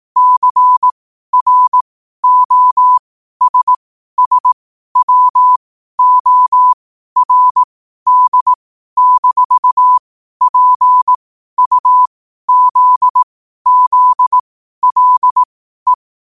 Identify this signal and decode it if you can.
Morse code, heard as a keyed tone, about 18 wpm, spaced as for 12 wpm, 1000 Hz. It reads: CROSSWORD-PUZZLE
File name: qotdfiles / 12wpm